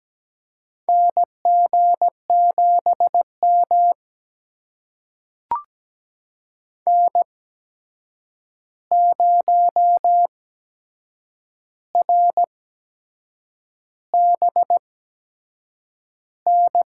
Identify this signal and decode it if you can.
Morse code, heard as a keyed tone, about 17 wpm, spaced as for 7 wpm, 700 Hz. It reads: NG7M N0RBN
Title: USA Call Signs - 8x Character Spacing - 17wpm